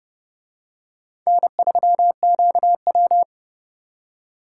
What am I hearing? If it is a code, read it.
Morse code, heard as a keyed tone, about 29 wpm, 700 Hz. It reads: N3QW